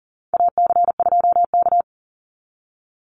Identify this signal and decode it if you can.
Morse code, heard as a keyed tone, about 40 wpm, 700 Hz. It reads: AC2K
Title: Call Signs Repeated 2x - 40wpm